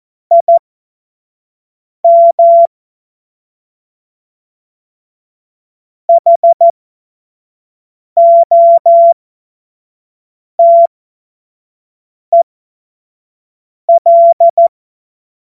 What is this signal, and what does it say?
Morse code, heard as a keyed tone, about 14 wpm, spaced as for 5 wpm, 680 Hz. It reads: IM HOTEL